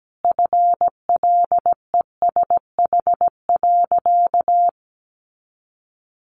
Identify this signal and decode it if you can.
Morse code, heard as a keyed tone, about 17 wpm, 700 Hz. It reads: FLESH.
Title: A Sound Of Thunder 17wpm